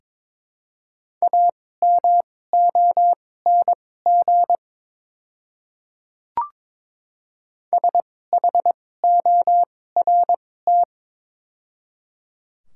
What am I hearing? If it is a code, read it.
Morse code, heard as a keyed tone, about 22 wpm, spaced as for 16 wpm, 700 Hz. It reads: AMONG SHORT
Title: 2x Character Spacing - Top 500 Words - 22wpm